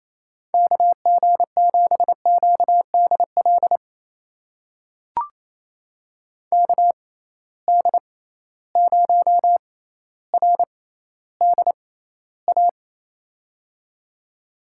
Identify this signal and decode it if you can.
Morse code, heard as a keyed tone, about 27 wpm, spaced as for 14 wpm, 700 Hz. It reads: KG7QDL KD0RDA
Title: USA Call Signs - 6 Characters - 6x Character Spacing - 28wpm